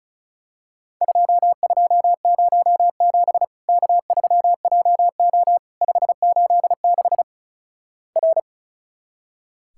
Morse code, heard as a keyed tone, about 36 wpm, 700 Hz. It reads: 2207 K3JO 586 R
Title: Worked All Europe DX Contest - QTC Exchange - 35wpm